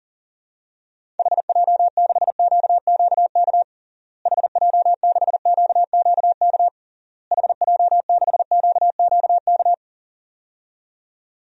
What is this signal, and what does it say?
Morse code, heard as a keyed tone, about 40 wpm, 700 Hz. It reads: HJ6QQK HJ6QQK HJ6QQK
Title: DXCC-100 Call Signs - 6 Characters - Repeated 3x - 40wpm